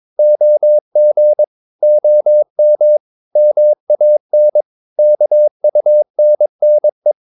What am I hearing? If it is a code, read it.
Morse code, heard as a keyed tone, about 22 wpm, 600 Hz. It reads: OG OM MAN KUNNE
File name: Ferie QRS 21wpm